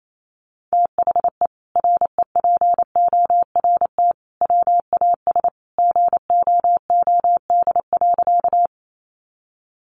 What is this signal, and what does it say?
Morse code, heard as a keyed tone, about 28 wpm, 700 Hz. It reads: THE REPORT WAS GOOD.
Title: Sentences from Top 900 Words 28wpm